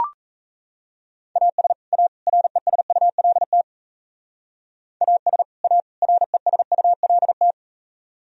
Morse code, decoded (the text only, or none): AS A RESULT AS A RESULT